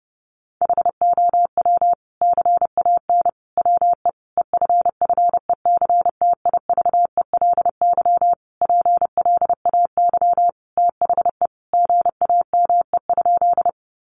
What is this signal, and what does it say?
Morse code, heard as a keyed tone, about 30 wpm, 700 Hz. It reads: HOW CAN WE EFFECTIVELY PLAY THE GAME?
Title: Sentences from Top 1400 Words 30wpm